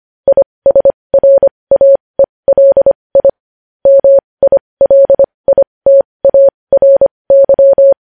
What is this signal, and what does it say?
Morse code, heard as a keyed tone, about 25 wpm, spaced as for 20 wpm, 550 Hz. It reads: ISRAELI MILITARY
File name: podcast / News headlines-1x-25x20